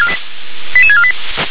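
Five-tone selective call (ZVEI), heard as quick runs of five tones, tones of about 70 ms. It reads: 88648
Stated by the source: ZVEI five-tone call 88915 received by SDR.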